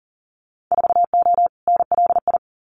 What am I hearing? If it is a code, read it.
Morse code, heard as a keyed tone, about 40 wpm, 700 Hz. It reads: 4O NLI
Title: Winter Field Day Exchanges Encore 40wpm